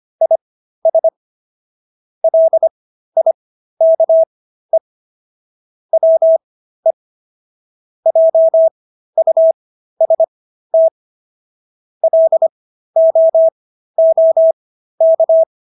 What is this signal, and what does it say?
Morse code, heard as a keyed tone, about 25 wpm, spaced as for 13 wpm, 650 Hz. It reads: IS LIKE WE JUST LOOK